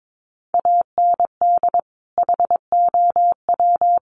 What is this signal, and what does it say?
Morse code, heard as a keyed tone, about 22 wpm, 700 Hz. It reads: AND HOW
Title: Sentences from Top 1200 Words 22wpm